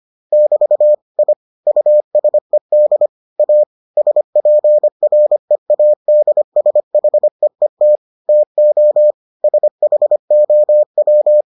Morse code, heard as a keyed tone, about 25 wpm, 600 Hz. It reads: <BT> I USED A SPREADSHEET TO SHOW